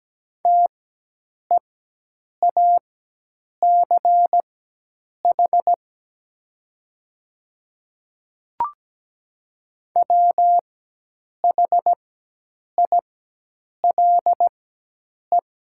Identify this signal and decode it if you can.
Morse code, heard as a keyed tone, about 17 wpm, spaced as for 8 wpm, 700 Hz. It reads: TEACH WHILE
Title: 4x Character Spacing - Top 500 Words - 17wpm